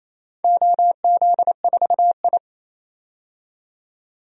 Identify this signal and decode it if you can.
Morse code, heard as a keyed tone, about 28 wpm, 700 Hz. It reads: OZ4I
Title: DXCC-100 Call Signs - 4 Characters - 28wpm